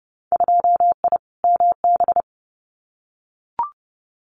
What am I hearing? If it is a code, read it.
Morse code, heard as a keyed tone, about 30 wpm, 700 Hz. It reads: 2I MB
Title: Winter Field Day Exchanges Encore 30wpm